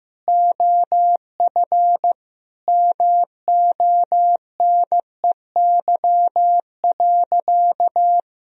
Morse code, heard as a keyed tone, about 15 wpm, 700 Hz. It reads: OF MONEY.
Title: Sentences from Top 1000 Words - Review Part 3 - 15wpm